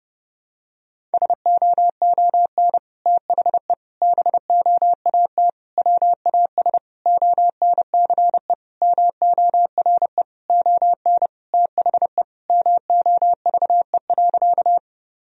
Morse code, heard as a keyed tone, about 32 wpm, 700 Hz. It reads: SOON THE BOAT WAS ONCE MORE ON THE MOVE.
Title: Sentences from Top 1000 Words - Review Part 3 - 30wpm